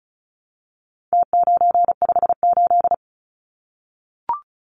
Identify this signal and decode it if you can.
Morse code, heard as a keyed tone, about 34 wpm, 700 Hz. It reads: T958
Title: Numbers - Contest Style - 4-Digit 35wpm